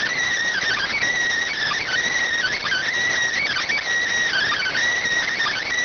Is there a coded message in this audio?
Yes: Emergency Alert System (SAME) tones (header bursts) from 0.5 to 4.8 s, heard as harsh buzzing bursts.